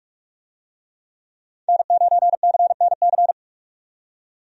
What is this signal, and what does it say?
Morse code, heard as a keyed tone, about 43 wpm, 700 Hz. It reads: N9CNC
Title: USA Call Signs - 5 Characters - 45wpm